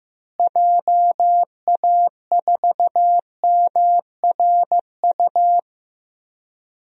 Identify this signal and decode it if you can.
Morse code, heard as a keyed tone, about 15 wpm, 700 Hz. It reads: JA4MRU
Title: DXCC-100 Call Signs - 6 Characters - Repeated 2x - 15wpm